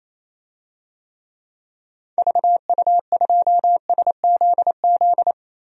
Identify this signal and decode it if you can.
Morse code, heard as a keyed tone, about 28 wpm, 700 Hz. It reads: VU2SZZ
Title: DXCC-100 Call Signs - 6 Characters - 28wpm